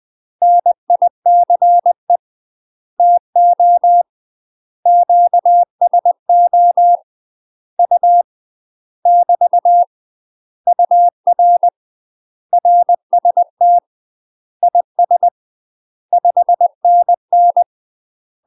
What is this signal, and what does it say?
Morse code, heard as a keyed tone, about 20 wpm, 700 Hz. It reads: NICE TO QSO U <BT> UR RST IS 5NN
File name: UK MASTER QSOs TRAINING / 20WPM / UK MASTER QSOs 0022